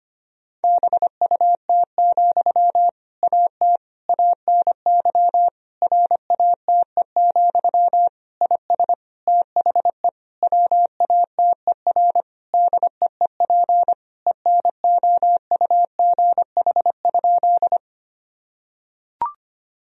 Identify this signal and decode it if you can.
Morse code, heard as a keyed tone, about 25 wpm, 700 Hz. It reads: BUT, AT ANY RATE, IS THE WATER DEEP ENOUGH?
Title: Sentences from Top 1000 Words - Review Part 4 - 25wpm